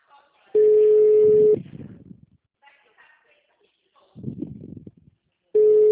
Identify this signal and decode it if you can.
Ringback tone (Europe, 425 Hz), with speech.